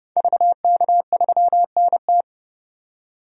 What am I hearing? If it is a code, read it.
Morse code, heard as a keyed tone, about 31 wpm, 700 Hz. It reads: VK3NT
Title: DXCC-100 Call Signs - 5 Characters - 30wpm